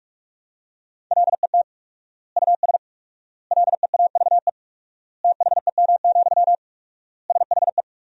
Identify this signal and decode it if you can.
Morse code, heard as a keyed tone, about 46 wpm, 700 Hz. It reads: LET US LEAVE THEM, SHE
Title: 4x Word-Spacing - Sentences from Top 400 Words - 45wpm